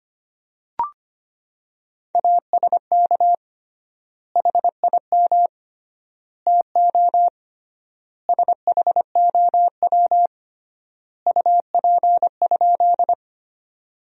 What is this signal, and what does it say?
Morse code, heard as a keyed tone, about 25 wpm, 700 Hz. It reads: ASK HIM TO SHOW UP?
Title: 3x Word-Spacing - Sentences from Top 200 Words - 25wpm